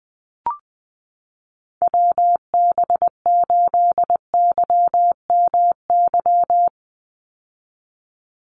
Morse code, heard as a keyed tone, about 20 wpm, 700 Hz. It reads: WB8YMY
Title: USA Call Signs 20wpm